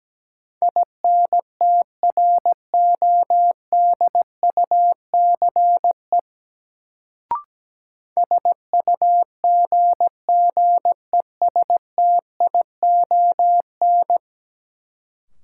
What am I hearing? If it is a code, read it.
Morse code, heard as a keyed tone, about 17 wpm, 700 Hz. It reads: INTRODUCE SUGGESTION
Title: Sentences from Top 1600 Words - New Word Drill - 17wpm